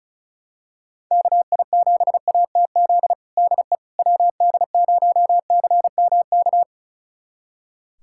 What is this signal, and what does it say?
Morse code, heard as a keyed tone, about 36 wpm, 700 Hz. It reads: KI7ATZ DE WD0CMK
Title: USA Callsign DE Callsign - 6 Characters - 35wpm